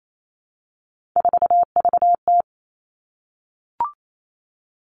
Morse code, heard as a keyed tone, about 28 wpm, 700 Hz. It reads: 4VT
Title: All Cut Numbers - 3-Digit 28wpm